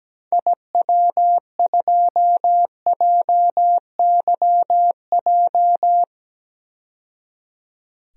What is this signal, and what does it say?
Morse code, heard as a keyed tone, about 17 wpm, 700 Hz. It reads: IW2JYJ
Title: DXCC-100 Call Signs - 6 Characters - 17wpm